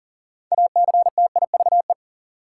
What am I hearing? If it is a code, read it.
Morse code, heard as a keyed tone, about 40 wpm, 700 Hz. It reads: ACTIVE
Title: Sentences from Top 900 Words - New Word Drill - 40wpm